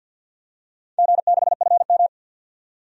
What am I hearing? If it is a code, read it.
Morse code, heard as a keyed tone, about 50 wpm, 700 Hz. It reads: G6FM
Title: DXCC-100 Call Signs - 4 Characters - 50wpm